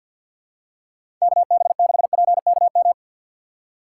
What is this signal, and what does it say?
Morse code, heard as a keyed tone, about 53 wpm, 700 Hz. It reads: KB6PKM
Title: USA Call Signs - 4x Character Spacing - 50wpm